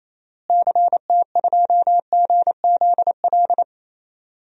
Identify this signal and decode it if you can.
Morse code, heard as a keyed tone, about 28 wpm, 700 Hz. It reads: CT2GZL
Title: DXCC-100 Call Signs - 6 Characters - Repeated 2x - 28wpm